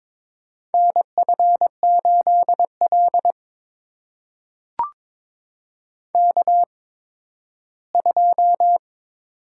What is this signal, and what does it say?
Morse code, heard as a keyed tone, about 22 wpm, spaced as for 13 wpm, 700 Hz. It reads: NF8L K2
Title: USA Call Signs - 8x Character Spacing - 22wpm